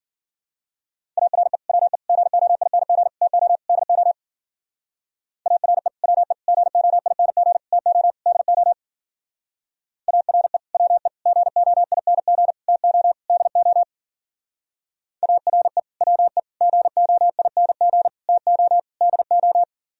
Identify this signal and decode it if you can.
Morse code, heard as a keyed tone, about 50 wpm, 700 Hz. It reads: ARE WE GOING TO DO ARE WE GOING TO DO ARE WE GOING TO DO ARE WE GOING TO DO
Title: Speed-Racing - Sets of 5 Words 40wpm